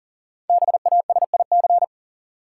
Morse code, heard as a keyed tone, about 40 wpm, 700 Hz. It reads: BASIC
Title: Sentences from Top 1000 Words - Review Part 3 - New Word Drill - 40wpm